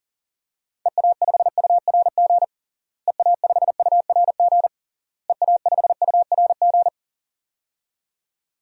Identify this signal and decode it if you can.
Morse code, heard as a keyed tone, about 40 wpm, 700 Hz. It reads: EA5URG EA5URG EA5URG
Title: DXCC-100 Call Signs - 6 Characters - Repeated 3x - 40wpm